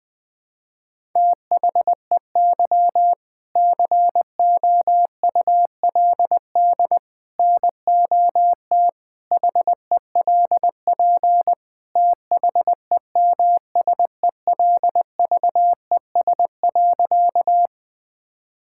Morse code, heard as a keyed tone, about 20 wpm, 700 Hz. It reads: THEY COULD NOT HELP THEMSELVES.
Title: Sentences from Top 1000 Words - Review Part 5 - 20wpm